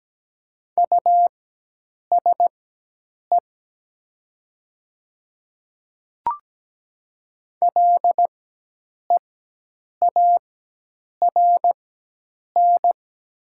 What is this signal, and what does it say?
Morse code, heard as a keyed tone, about 17 wpm, spaced as for 8 wpm, 700 Hz. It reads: USE LEARN
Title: 4x Character Spacing - Top 300 Words - 17wpm